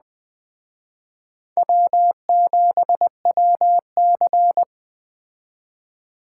Morse code, heard as a keyed tone, about 20 wpm, 700 Hz. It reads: W7WC